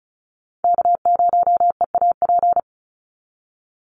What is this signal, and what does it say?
Morse code, heard as a keyed tone, about 34 wpm, 700 Hz. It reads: K0EAP
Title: NA SOTA Chaser Callsigns 35wpm